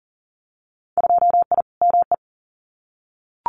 Morse code, heard as a keyed tone, about 41 wpm, 700 Hz. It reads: 2I ME
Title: Winter Field Day Exchanges Encore 40wpm